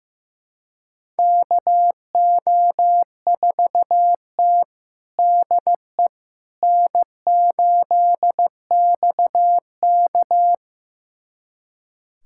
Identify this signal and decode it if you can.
Morse code, heard as a keyed tone, about 16 wpm, 700 Hz. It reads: KO4T DE N8XK
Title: USA Callsign DE Callsign - 4 Characters - 15wpm